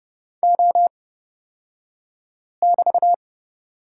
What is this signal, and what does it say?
Morse code, heard as a keyed tone, about 29 wpm, spaced as for 5 wpm, 690 Hz. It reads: O<BT>
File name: Lesson 36d - BT - Two Characters 30wpm